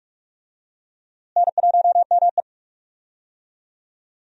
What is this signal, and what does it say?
Morse code, heard as a keyed tone, about 41 wpm, 700 Hz. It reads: N1ME
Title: USA Call Signs - 4 Characters - Repeated 3x - 45wpm